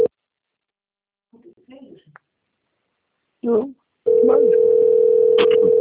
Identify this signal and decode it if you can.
Ringback tone (North America), 440 + 480 Hz, with speech.
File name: rg-1600-unknown- - - .17343